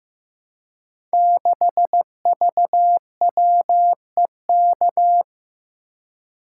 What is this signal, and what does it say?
Morse code, heard as a keyed tone, about 15 wpm, 700 Hz. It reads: 6VWEK
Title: Code Groups - 5 Letter:Numbers - 15wpm